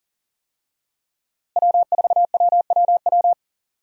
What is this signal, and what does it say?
Morse code, heard as a keyed tone, about 40 wpm, 700 Hz. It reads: W4WWW
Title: USA Call Signs - 5 Characters - Repeated 2x - 40wpm